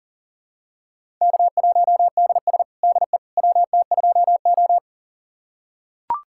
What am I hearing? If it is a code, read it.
Morse code, heard as a keyed tone, about 40 wpm, 700 Hz. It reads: K1DS DE WT2O
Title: USA Callsign DE Callsign - 4 Characters - 40wpm